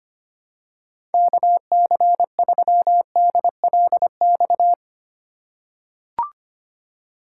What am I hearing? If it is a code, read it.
Morse code, heard as a keyed tone, about 25 wpm, 700 Hz. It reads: KC3DLX
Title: USA Call Signs - 6 Characters - 2x Character Spacing - 25wpm